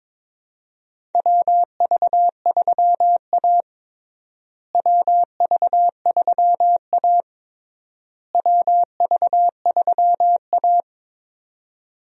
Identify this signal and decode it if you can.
Morse code, heard as a keyed tone, about 22 wpm, 700 Hz. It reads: WV3A WV3A WV3A
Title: USA Call Signs - 4 Characters - Repeated 3x - 22wpm